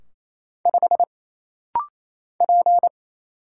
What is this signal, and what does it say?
Morse code, heard as a keyed tone, about 27 wpm, 700 Hz. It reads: 5 P
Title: Single Letter-Number - Rapid-Fire 28wpm